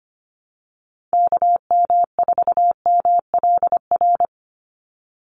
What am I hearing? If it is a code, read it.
Morse code, heard as a keyed tone, about 25 wpm, 700 Hz. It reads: KM4MLR
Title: USA Call Signs 25wpm